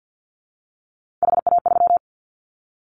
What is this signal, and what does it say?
Morse code, heard as a keyed tone, about 50 wpm, 700 Hz. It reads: HA3
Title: Callsign Prefixes for Top 25 DXCC - 50wpm